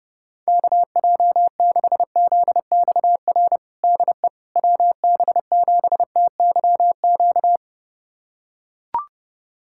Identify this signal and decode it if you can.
Morse code, heard as a keyed tone, about 30 wpm, 700 Hz. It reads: KJ6ZXR DE WB7TYQ
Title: USA Callsign DE Callsign - 6 Characters - 30wpm